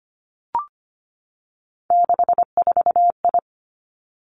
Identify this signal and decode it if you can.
Morse code, heard as a keyed tone, about 25 wpm, 700 Hz. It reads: 64I
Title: Three Challenging Letter-Number Combinations 25wpm